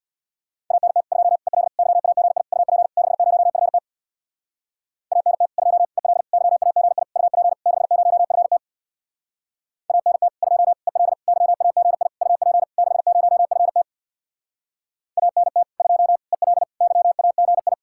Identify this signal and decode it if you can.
Morse code, heard as a keyed tone, about 67 wpm, 700 Hz. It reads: ANT 2 EL YAGI UP 60FT ANT 2 EL YAGI UP 60FT ANT 2 EL YAGI UP 60FT ANT 2 EL YAGI